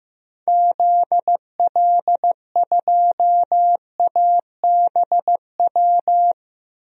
Morse code, heard as a keyed tone, about 15 wpm, 700 Hz. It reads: ZL2ABW
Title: DXCC-100 Call Signs - 6 Characters - 15wpm